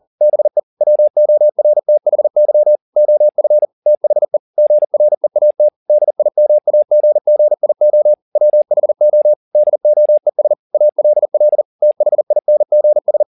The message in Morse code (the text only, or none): BE WORTHY OF THE GREAT DIMAGGIO WHO DOES ALL THINGS